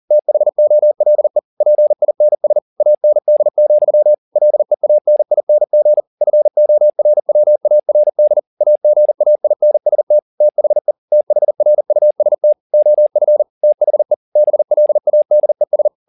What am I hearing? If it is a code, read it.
Morse code, heard as a keyed tone, about 40 wpm, 600 Hz. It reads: THOLE PINS AND, LEANING FORWARD AGAINST THE THRUST OF THE BLADES